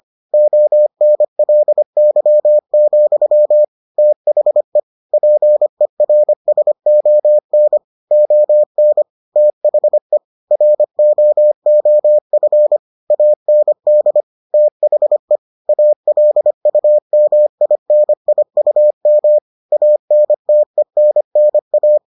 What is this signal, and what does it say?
Morse code, heard as a keyed tone, about 25 wpm, 600 Hz. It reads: ONLY, THE PERSON ON THE ROOF AND THE ALUMINIUM ANTENNA